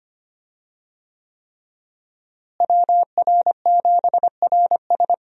Morse code, heard as a keyed tone, about 25 wpm, 700 Hz. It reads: WR7RS
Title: USA Call Signs - 5 Characters - Repeated 3x - 25wpm